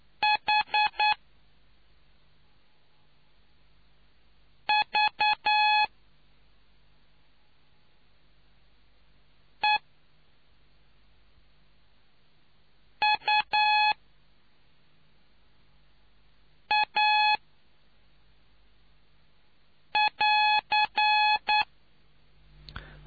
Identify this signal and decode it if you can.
Morse code, heard as a keyed tone, about 9 wpm, spaced as for 2 wpm, 850 Hz. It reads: HVEUA<AR>